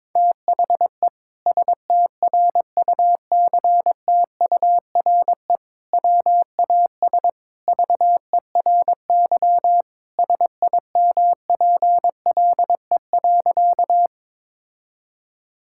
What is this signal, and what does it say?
Morse code, heard as a keyed tone, about 22 wpm, 700 Hz. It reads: THE STRUCTURE WAS VERY SIMPLE.
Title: Sentences from Top 1000 Words - Review Part 6 - 22wpm